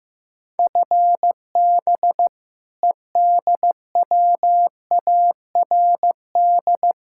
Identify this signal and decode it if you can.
Morse code, heard as a keyed tone, about 15 wpm, 700 Hz. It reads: FB EDWARD
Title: Practice QSO - Exchange 3 - More Info - 15wpm